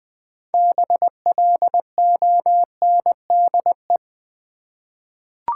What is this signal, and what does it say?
Morse code, heard as a keyed tone, about 20 wpm, 700 Hz. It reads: BLONDE
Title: 6 Letter Words - First 3 Letters Given - 20wpm